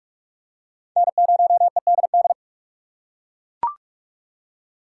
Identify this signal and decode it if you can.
Morse code, heard as a keyed tone, about 38 wpm, 700 Hz. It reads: N0EDD